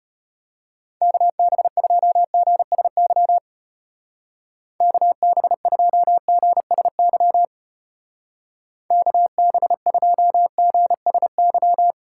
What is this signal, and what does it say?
Morse code, heard as a keyed tone, about 33 wpm, 700 Hz. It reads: KB2GSY KB2GSY KB2GSY